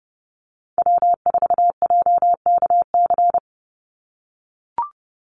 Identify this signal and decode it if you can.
Morse code, heard as a keyed tone, about 29 wpm, 700 Hz. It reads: W4JKC
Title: NA SOTA Chaser Callsigns 30wpm